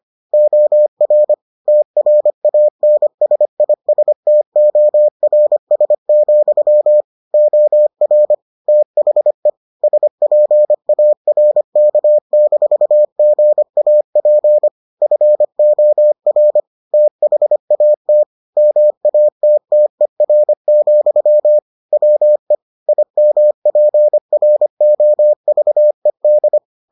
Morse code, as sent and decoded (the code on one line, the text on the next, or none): --- .-. / - .-. .- -. ... .. ... - --- .-. ... --..-- / --- .-. / - .... / ... .--. .- .-. -.- -....- --. .- .--. / ..-. --- .-. / - .... .- - / -- .- - - . .-. --..-- / .-- . / .. -- .--. .-. --- ...- . -..
OR TRANSISTORS, OR THE SPARK-GAP FOR THAT MATTER, WE IMPROVED